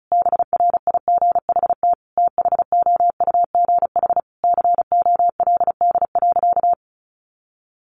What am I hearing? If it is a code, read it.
Morse code, heard as a keyed tone, about 34 wpm, 700 Hz. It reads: BRIGHT THOUGH COLD.
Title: Sentences from Top 1000 Words 35wpm